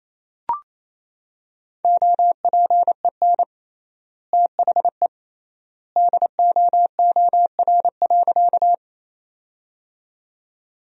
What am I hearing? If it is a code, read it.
Morse code, heard as a keyed tone, about 28 wpm, 700 Hz. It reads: OPEN THE DOOR.
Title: 3x Word-Spacing - Sentences from Top 300 Words - 28wpm